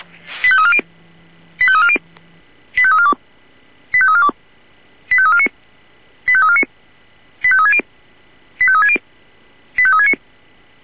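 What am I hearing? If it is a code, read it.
Five-tone selective call (ZVEI), heard as quick runs of five tones, tones of about 70 ms. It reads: 85340 85340 85342 85342 85349 85349 85360 85360 85369
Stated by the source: ZVEI five-tone selective call received by SDR.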